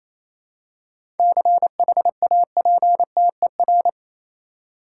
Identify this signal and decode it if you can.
Morse code, heard as a keyed tone, about 28 wpm, 700 Hz. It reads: CHAPTER
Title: Seven Letter Words - 28wpm